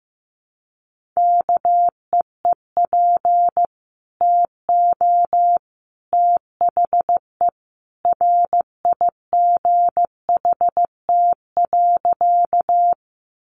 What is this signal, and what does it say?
Morse code, heard as a keyed tone, about 15 wpm, 700 Hz. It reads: KEEP TO THE RIGHT.
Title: Sentences from Top 300 Words 15wpm